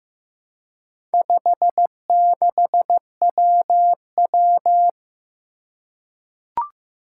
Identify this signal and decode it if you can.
Morse code, heard as a keyed tone, about 15 wpm, 700 Hz. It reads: 56WW
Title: Four Challenging Letter-Number Combinations 15wpm